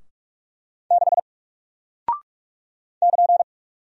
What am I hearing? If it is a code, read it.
Morse code, heard as a keyed tone, about 42 wpm, 700 Hz. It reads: 6 <KN>
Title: Single Character (Letter:Number:Punctuation:Prosign) Alt 45wpm